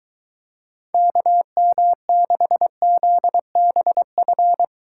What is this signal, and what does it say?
Morse code, heard as a keyed tone, about 23 wpm, 700 Hz. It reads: KM6ZBF